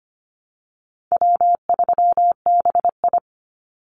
Morse code, heard as a keyed tone, about 25 wpm, 700 Hz. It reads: W3BI
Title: Easy Call Signs Repeated 2x - 25wpm